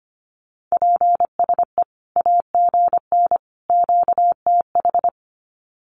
Morse code, heard as a keyed tone, about 25 wpm, 700 Hz. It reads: PSE AGN QTH
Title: Common QSO Phrases 25wpm